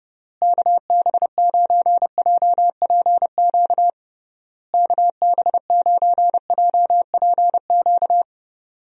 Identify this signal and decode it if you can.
Morse code, heard as a keyed tone, about 28 wpm, 700 Hz. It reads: KB9JPQ KB9JPQ
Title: USA Call Signs - 6 Characters - Repeated 3x - 30wpm